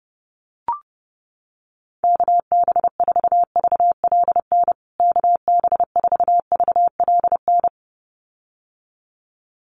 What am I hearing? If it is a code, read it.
Morse code, heard as a keyed tone, about 30 wpm, 700 Hz. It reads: KB4VLN KB4VLN
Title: USA Call Signs Repeated 2x 30wpm